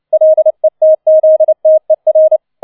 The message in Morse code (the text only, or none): LETZTER